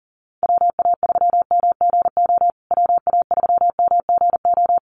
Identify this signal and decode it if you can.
Morse code, heard as a keyed tone, about 40 wpm, 700 Hz. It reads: WA3MGO WA3MGO
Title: USA Call Signs Repeated 3x 40wpm